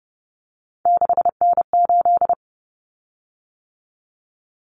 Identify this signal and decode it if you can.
Morse code, heard as a keyed tone, about 30 wpm, 700 Hz. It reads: 6N8
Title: Common Cut Numbers - 3-Digit 30wpm